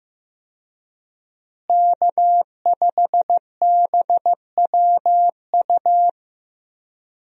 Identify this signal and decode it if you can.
Morse code, heard as a keyed tone, about 15 wpm, 700 Hz. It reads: K5BWU
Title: USA Call Signs - 5 Characters - Repeated 2x - 15wpm